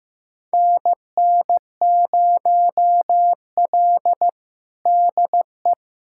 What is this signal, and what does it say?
Morse code, heard as a keyed tone, about 15 wpm, 700 Hz. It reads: NN0L DE
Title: USA Callsign DE Callsign - 4 Characters - 15wpm